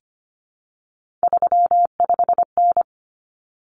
Morse code, heard as a keyed tone, about 25 wpm, 700 Hz. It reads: 35N
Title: Common Cut Numbers - 3-Digit 25wpm